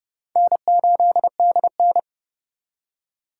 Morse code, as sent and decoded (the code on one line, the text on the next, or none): -. ---.. -.. -.
N8DN